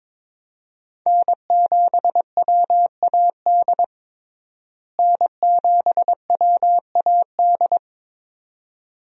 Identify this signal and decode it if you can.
Morse code, heard as a keyed tone, about 22 wpm, 700 Hz. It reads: N7WAD N7WAD
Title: USA Call Signs - 5 Characters - Repeated 2x - 22wpm